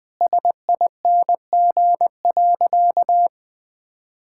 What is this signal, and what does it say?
Morse code, heard as a keyed tone, about 20 wpm, 700 Hz. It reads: SING.
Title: Sentences from Top 1600 Words 20wpm